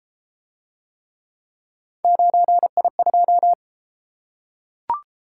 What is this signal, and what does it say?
Morse code, heard as a keyed tone, about 32 wpm, 700 Hz. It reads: 9I2